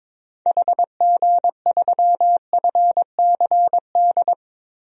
Morse code, heard as a keyed tone, about 22 wpm, 700 Hz. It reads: HG3FCD
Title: DXCC-100 Call Signs - 6 Characters - Repeated 2x - 22wpm